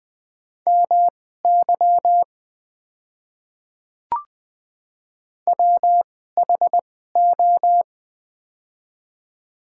Morse code, heard as a keyed tone, about 20 wpm, spaced as for 14 wpm, 700 Hz. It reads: MY WHO
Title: 2x Character Spacing - Top 200 Words - 20wpm